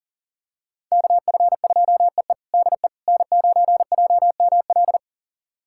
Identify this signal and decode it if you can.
Morse code, heard as a keyed tone, about 40 wpm, 700 Hz. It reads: KF2EE DE N9JML
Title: USA Callsign DE Callsign - 5 Characters - 40wpm